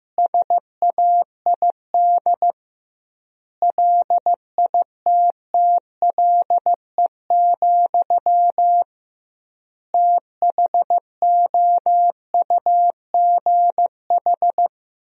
Morse code, heard as a keyed tone, about 15 wpm, 700 Hz. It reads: SAID LITTLE, THOUGH
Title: 2x Word-Spacing - Sentences from Top 400 Words - 15wpm